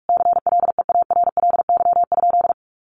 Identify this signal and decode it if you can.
Morse code, heard as a keyed tone, about 44 wpm, 700 Hz. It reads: CLEARLY?
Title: Sentences from Top 1300 Words 45wpm